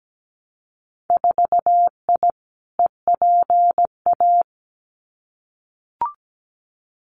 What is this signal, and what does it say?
Morse code, heard as a keyed tone, about 17 wpm, 700 Hz. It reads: 4I EPA
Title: Winter Field Day Exchanges Encore 17wpm